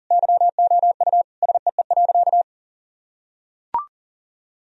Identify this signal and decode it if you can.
Morse code, heard as a keyed tone, about 40 wpm, 700 Hz. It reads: YOU SEE.
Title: Sentences from Top 1000 Words - Review Part 2 - 40wpm